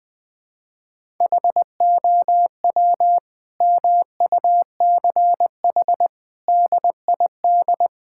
Morse code, heard as a keyed tone, about 21 wpm, 700 Hz. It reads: HOW MUCH DID
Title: Sentences from Top 1000 Words - Review Part 6 - 20wpm